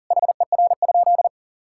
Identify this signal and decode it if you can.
Morse code, heard as a keyed tone, about 40 wpm, 700 Hz. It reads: HER?
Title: Sentences from Top 1600 Words 40wpm